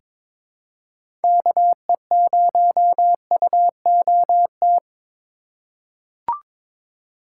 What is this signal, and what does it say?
Morse code, heard as a keyed tone, about 22 wpm, 700 Hz. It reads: KE0UOT
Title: USA Call Signs - 6 Characters - 2x Character Spacing - 22wpm